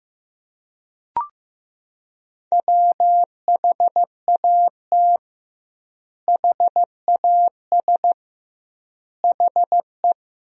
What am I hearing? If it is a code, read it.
Morse code, heard as a keyed tone, about 15 wpm, 700 Hz. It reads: WHAT HAS HE
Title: 2x Word-Spacing - Sentences from Top 500 Words - 15wpm